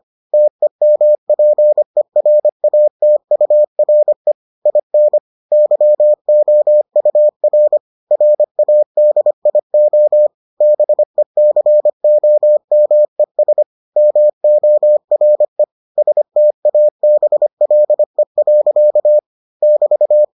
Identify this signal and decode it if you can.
Morse code, heard as a keyed tone, about 25 wpm, 600 Hz. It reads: TEMPERATURE IN YOUR RADIO BECOMES MORE STABLE. <BT>